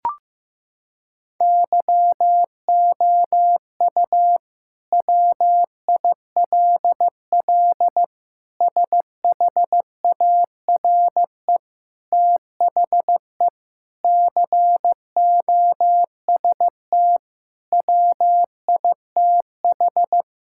Morse code, heard as a keyed tone, about 16 wpm, 700 Hz. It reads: YOU WILL SHARE THE COST WITH
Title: Sentences from Top 1000 Words - Review Part 6 - 15wpm